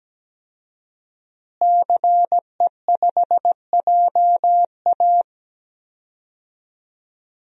Morse code, heard as a keyed tone, about 17 wpm, 700 Hz. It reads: CE5JA
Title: DXCC-100 Call Signs - 5 Characters - Repeated 3x - 17wpm